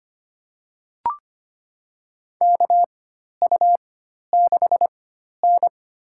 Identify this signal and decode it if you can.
Morse code, heard as a keyed tone, about 25 wpm, spaced as for 12 wpm, 700 Hz. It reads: KU6N
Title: USA Call Signs - 4 Characters - 4x Character Spacing - 25wpm